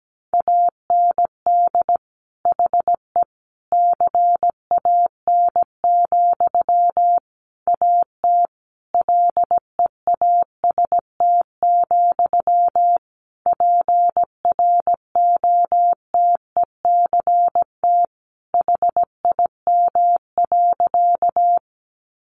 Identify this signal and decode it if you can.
Morse code, heard as a keyed tone, about 17 wpm, 700 Hz. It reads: AND HE CAN, AT LEAST, PROTECT HIM.
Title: Sentences from Top 900 Words 17wpm